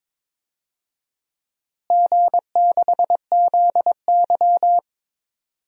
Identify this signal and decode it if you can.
Morse code, heard as a keyed tone, about 22 wpm, 700 Hz. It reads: G6ZY
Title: DXCC-100 Call Signs - 4 Characters - Repeated 2x - 22wpm